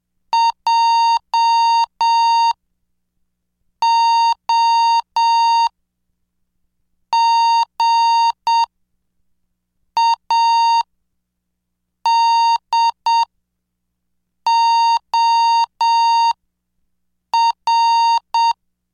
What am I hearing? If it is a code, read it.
Morse code, heard as a keyed tone, about 7 wpm, spaced as for 5 wpm, 920 Hz. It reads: JOGADOR